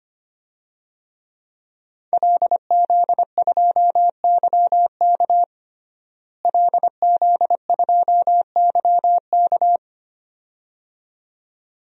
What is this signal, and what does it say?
Morse code, heard as a keyed tone, about 25 wpm, 700 Hz. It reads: LZ2YK LZ2YK
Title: DXCC-100 Call Signs - 5 Characters - Repeated 2x - 25wpm